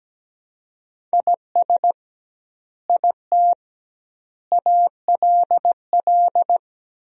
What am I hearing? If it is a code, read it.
Morse code, heard as a keyed tone, about 17 wpm, 700 Hz. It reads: IS IT ALL
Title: 2x Word-Spacing - Sentences from Top 200 Words - 17wpm